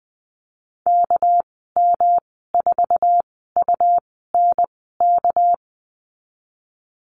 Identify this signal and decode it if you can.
Morse code, heard as a keyed tone, about 20 wpm, spaced as for 14 wpm, 700 Hz. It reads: KM4UNK
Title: 231 Lesson 37g - Letter x - Call Signs 20wpm